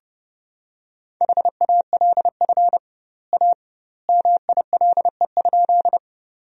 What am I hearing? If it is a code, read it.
Morse code, heard as a keyed tone, about 30 wpm, 700 Hz. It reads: HALF A MILE?
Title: 2x Word-Spacing - Sentences from Top 400 Words - 30wpm